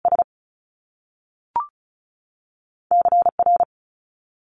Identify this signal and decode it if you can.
Morse code, heard as a keyed tone, about 36 wpm, 700 Hz. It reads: S C R
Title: Top Two Letter Combinations in Words 35wpm